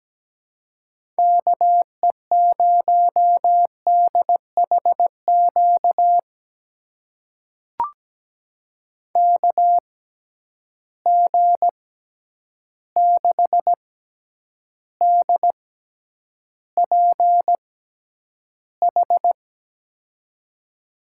Morse code, heard as a keyed tone, about 17 wpm, spaced as for 9 wpm, 700 Hz. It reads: KE0DHQ KG6DPH
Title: USA Call Signs - 6 Characters - 6x Character Spacing - 17wpm